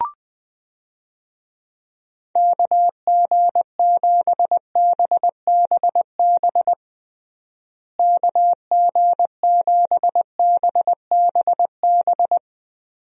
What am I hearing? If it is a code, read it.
Morse code, heard as a keyed tone, about 20 wpm, 700 Hz. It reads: KG7BBB KG7BBB
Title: USA Call Signs - 6 Characters - Repeated 2x - 20wpm